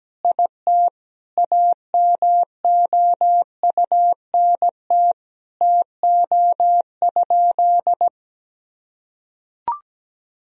Morse code, heard as a keyed tone, about 17 wpm, 700 Hz. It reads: IT AMOUNT TO?
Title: Sentences from Top 1000 Words - Review Part 4 - 17wpm